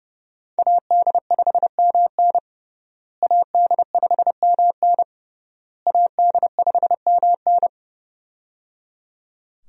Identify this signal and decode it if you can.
Morse code, heard as a keyed tone, about 29 wpm, 700 Hz. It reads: AD5MN AD5MN AD5MN
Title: USA Call Signs - 5 Characters - Repeated 3x - 30wpm